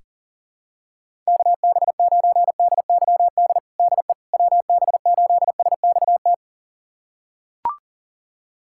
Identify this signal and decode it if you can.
Morse code, heard as a keyed tone, about 40 wpm, 700 Hz. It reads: KB9DYD DE WB8SXT
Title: USA Callsign DE Callsign - 6 Characters - 40wpm